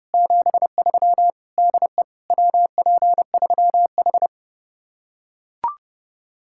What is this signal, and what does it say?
Morse code, heard as a keyed tone, about 32 wpm, 700 Hz. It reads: 73 DE WP3H